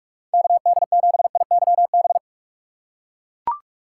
Morse code, heard as a keyed tone, about 43 wpm, 700 Hz. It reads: KD7IYB
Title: USA Call Signs - 6 Characters - 6x Character Spacing - 45wpm